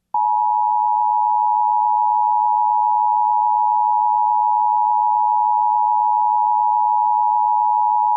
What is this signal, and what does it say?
Emergency Alert System (SAME) tones (the 853 and 960 Hz attention tone). It is heard from 0.8 to 7.7 s.